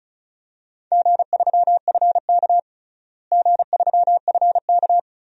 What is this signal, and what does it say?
Morse code, heard as a keyed tone, about 34 wpm, 700 Hz. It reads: G3FK G3FK
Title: DXCC-100 Call Signs - 4 Characters - Repeated 2x - 35wpm